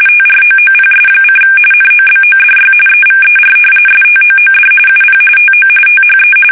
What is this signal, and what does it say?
RTTY, about 45.45 baud. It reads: TROOPS: NOBODY EVER STOOD UP TO THE
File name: fsk850 / battleshipnj